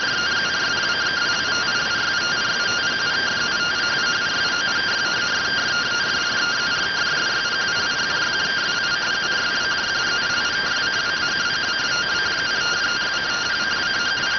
Two-tone teleprinter signal, about 170 Hz shift.